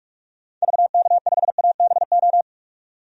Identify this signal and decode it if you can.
Morse code, heard as a keyed tone, about 46 wpm, 700 Hz. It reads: VK5ABO
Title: DXCC-100 Call Signs - 6 Characters - 45wpm